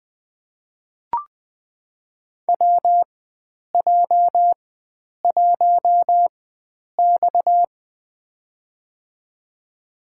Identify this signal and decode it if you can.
Morse code, heard as a keyed tone, about 20 wpm, spaced as for 9 wpm, 700 Hz. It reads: WJ1X
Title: USA Call Signs - 4 Characters - 4x Character Spacing - 20wpm